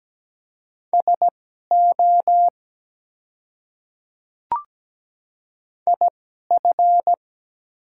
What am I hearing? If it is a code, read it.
Morse code, heard as a keyed tone, about 17 wpm, spaced as for 12 wpm, 700 Hz. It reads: SO IF
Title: 2x Character Spacing - Top 100 Words - 17wpm